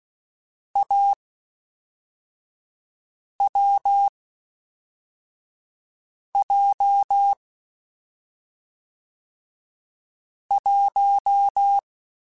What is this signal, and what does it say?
Morse code, heard as a keyed tone, about 16 wpm, 800 Hz. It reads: A W J 1